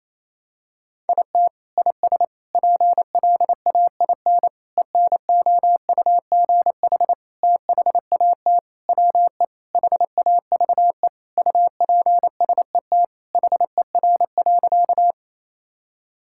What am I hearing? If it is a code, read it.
Morse code, heard as a keyed tone, about 28 wpm, 700 Hz. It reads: IT IS PLAIN ENOUGH THAT WE HAVE UPSET HER.
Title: Sentences from Top 1000 Words - Review Part 5 - 28wpm